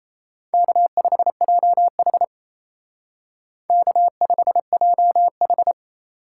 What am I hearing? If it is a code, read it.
Morse code, heard as a keyed tone, about 35 wpm, 700 Hz. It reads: K5JH K5JH
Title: Speed Racing - USA 1x2 Call Signs 28wpm